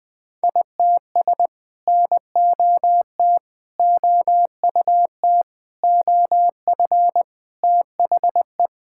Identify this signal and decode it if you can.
Morse code, heard as a keyed tone, about 20 wpm, 700 Hz. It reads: ITS NOT OUT OF THE
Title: Sets of 9 Words - 20wpm